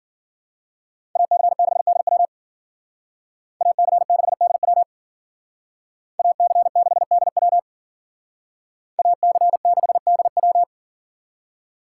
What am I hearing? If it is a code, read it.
Morse code, heard as a keyed tone, about 52 wpm, 700 Hz. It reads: AC6DW AC6DW AC6DW AC6DW